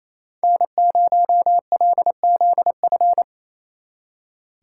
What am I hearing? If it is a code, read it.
Morse code, heard as a keyed tone, about 36 wpm, 700 Hz. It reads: N0LZF